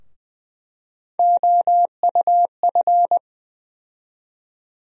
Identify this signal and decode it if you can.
Morse code, heard as a keyed tone, about 20 wpm, 700 Hz. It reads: OUF
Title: USA Call Sign Postfixes - 3 Letters 20wpm